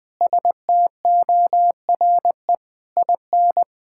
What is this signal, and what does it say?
Morse code, heard as a keyed tone, about 21 wpm, 700 Hz. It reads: STORE IN
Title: Sentences from Top 1000 Words - Review Part 5 - 20wpm